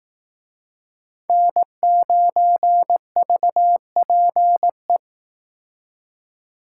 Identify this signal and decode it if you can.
Morse code, heard as a keyed tone, about 18 wpm, 700 Hz. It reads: N9VPE
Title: Speed Racing - USA 1x3 Call Signs 15wpm